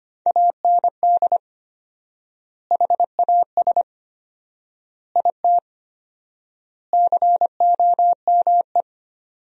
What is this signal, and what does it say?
Morse code, heard as a keyed tone, about 25 wpm, 700 Hz. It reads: AND HAS IT COME
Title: 4x Word-Spacing - Sentences from Top 100 Words - 25wpm